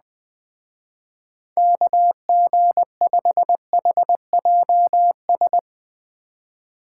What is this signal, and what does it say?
Morse code, heard as a keyed tone, about 20 wpm, 700 Hz. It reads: KG5HJS